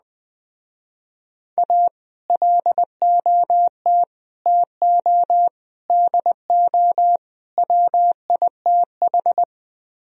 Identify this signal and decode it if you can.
Morse code, heard as a keyed tone, about 20 wpm, 700 Hz. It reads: A LOT TO DO WITH